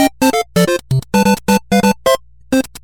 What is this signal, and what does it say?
There is no code.